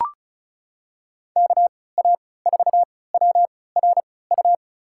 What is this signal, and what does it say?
Morse code, heard as a keyed tone, about 36 wpm, spaced as for 20 wpm, 700 Hz. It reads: KA4WRU
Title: USA Call Signs - 6 Characters - 3x Character Spacing - 35wpm